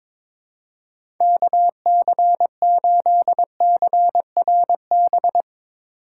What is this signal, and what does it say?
Morse code, heard as a keyed tone, about 22 wpm, 700 Hz. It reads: KC8CRB
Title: USA Call Signs - 6 Characters - 8x Character Spacing - 22wpm